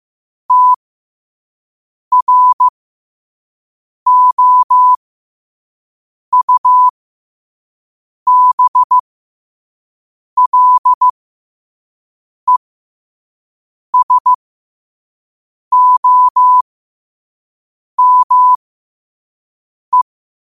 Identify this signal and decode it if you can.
Morse code, heard as a keyed tone, about 15 wpm, spaced as for 5 wpm, 1000 Hz. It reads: TROUBLESOME
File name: qotdfiles / 5wpm